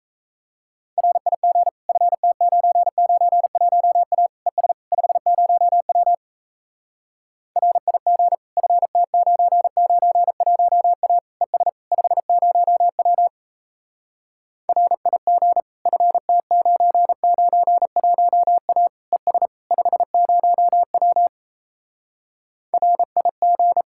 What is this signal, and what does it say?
Morse code, heard as a keyed tone, about 37 wpm, 700 Hz. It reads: RIG FT991A ES 50W RIG FT991A ES 50W RIG FT991A ES 50W RIG